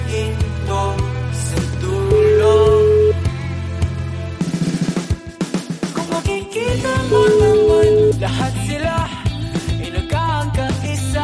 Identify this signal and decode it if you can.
Ringback tone (Europe, 425 Hz), with noise.